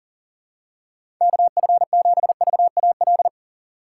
Morse code, heard as a keyed tone, about 37 wpm, 700 Hz. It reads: KF7VAL
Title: USA Call Signs - 6 Characters - 4x Character Spacing - 40wpm